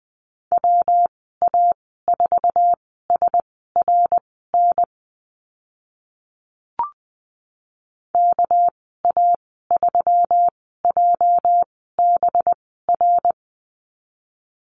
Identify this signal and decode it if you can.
Morse code, heard as a keyed tone, about 20 wpm, spaced as for 14 wpm, 700 Hz. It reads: WA4SRN KA3JBR
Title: 219 Lesson 35g - Number 0 - Call Signs 20wpm